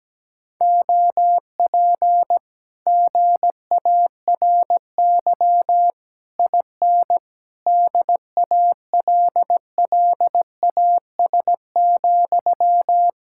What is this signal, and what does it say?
Morse code, heard as a keyed tone, about 17 wpm, 700 Hz. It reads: OP GARY IN DALLAS,
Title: Speed-Racing Common QSO Phrases - 17wpm